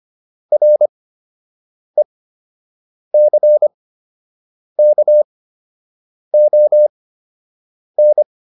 Morse code, heard as a keyed tone, about 25 wpm, spaced as for 7 wpm, 600 Hz. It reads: RECKON